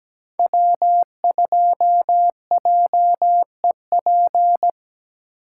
Morse code, heard as a keyed tone, about 20 wpm, 700 Hz. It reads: W2JEP
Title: Speed Racing - USA 1x3 Call Signs 17wpm